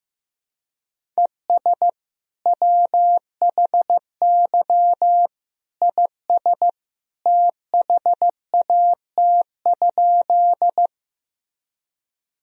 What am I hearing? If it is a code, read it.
Morse code, heard as a keyed tone, about 15 wpm, 700 Hz. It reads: ES WHY IS THAT?
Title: Sentences from Top 200 Words and Abbreviations 15wpm